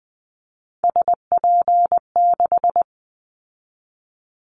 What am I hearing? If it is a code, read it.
Morse code, heard as a keyed tone, about 20 wpm, 700 Hz. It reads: SP6
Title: Callsign Prefixes for Top 25 DXCC - 20wpm